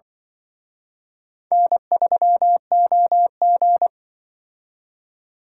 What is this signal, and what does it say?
Morse code, heard as a keyed tone, about 24 wpm, 700 Hz. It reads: N3OG
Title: Speed Racing - USA 1x2 Call Signs 20wpm